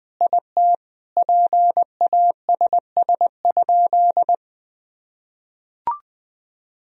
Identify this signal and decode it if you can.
Morse code, heard as a keyed tone, about 20 wpm, 700 Hz. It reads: IT PASS?
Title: Sentences from Top 400 Words and Abbreviations 20wpm